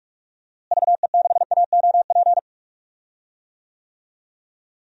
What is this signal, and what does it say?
Morse code, heard as a keyed tone, about 42 wpm, 700 Hz. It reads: VE6AOP
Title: DXCC-100 Call Signs - 6 Characters - Repeated 2x - 45wpm